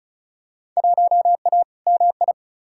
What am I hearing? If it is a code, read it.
Morse code, heard as a keyed tone, about 34 wpm, 700 Hz. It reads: 1A MI
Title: Field Day Exchanges - 35wpm